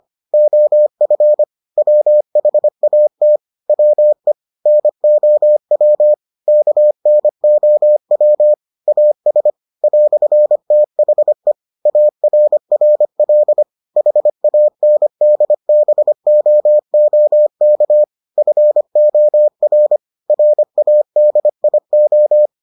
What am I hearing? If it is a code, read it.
Morse code, heard as a keyed tone, about 25 wpm, 600 Hz. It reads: OF WHAT WE NOW KNOW AS "THE ARRL HANDBOOK FOR RADIO